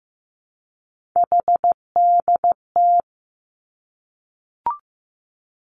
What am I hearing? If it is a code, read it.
Morse code, heard as a keyed tone, about 15 wpm, 700 Hz. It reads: HDT
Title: Three Challenging Letter-Number Combinations 15wpm